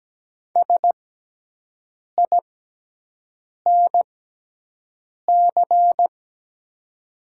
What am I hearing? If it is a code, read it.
Morse code, heard as a keyed tone, about 17 wpm, spaced as for 6 wpm, 700 Hz. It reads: SINC
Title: 6x Character Spacing - Top 500 Words - 17wpm